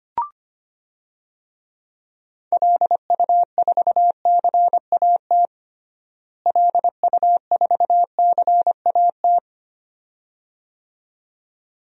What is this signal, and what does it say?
Morse code, heard as a keyed tone, about 25 wpm, 700 Hz. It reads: LU4CAT LU4CAT
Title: DXCC-100 Call Signs - 6 Characters - Repeated 2x - 25wpm